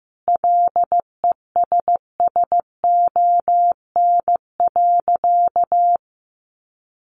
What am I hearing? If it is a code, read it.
Morse code, heard as a keyed tone, about 15 wpm, 700 Hz. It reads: LESSON.
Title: Sentences from Top 1200 Words 15wpm